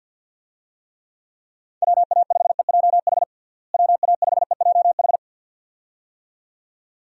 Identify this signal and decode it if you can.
Morse code, heard as a keyed tone, about 53 wpm, 700 Hz. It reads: WA5EJH WA5EJH
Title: USA Call Signs - 6 Characters - Repeated 2x - 50wpm